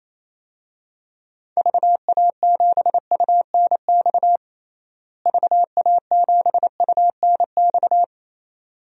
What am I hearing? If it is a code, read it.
Morse code, heard as a keyed tone, about 28 wpm, 700 Hz. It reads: VA7UNX VA7UNX
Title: DXCC-100 Call Signs - 6 Characters - Repeated 2x - 28wpm